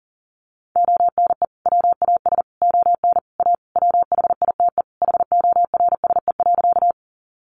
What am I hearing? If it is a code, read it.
Morse code, heard as a keyed tone, about 40 wpm, 700 Hz. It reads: ONE WAS ON A WHITE HORSE.
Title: Sentences from Top 300 Words 40wpm